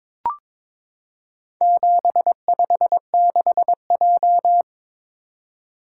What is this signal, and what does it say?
Morse code, heard as a keyed tone, about 22 wpm, 700 Hz. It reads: 756J
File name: practice / 22wpm / Four Letter-Number Combinations - 22wpm